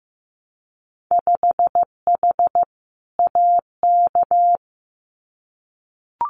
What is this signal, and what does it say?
Morse code, heard as a keyed tone, about 15 wpm, 700 Hz. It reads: 5H AK
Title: Winter Field Day Exchanges Encore 15wpm